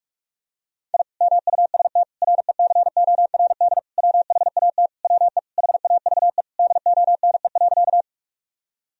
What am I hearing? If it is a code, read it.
Morse code, heard as a keyed tone, about 41 wpm, 700 Hz. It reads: I MUST RECORD WHAT WE HAVE DONE.
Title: Sentences from Top 1000 Words - Review Part 3 - 45wpm